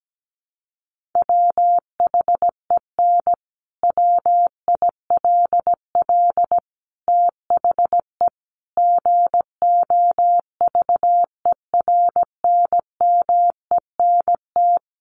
Morse code, heard as a keyed tone, about 17 wpm, 700 Hz. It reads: WHEN WILL THE GOVERNMENT
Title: Sentences from Top 1400 Words 17wpm